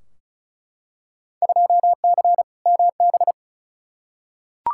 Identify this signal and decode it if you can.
Morse code, heard as a keyed tone, about 36 wpm, 700 Hz. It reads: 2C MB
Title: Field Day Exchanges Encore - 35wpm